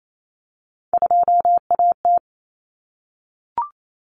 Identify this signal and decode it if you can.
Morse code, heard as a keyed tone, about 28 wpm, 700 Hz. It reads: 2AT
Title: Common Cut Numbers - 3-Digit 28wpm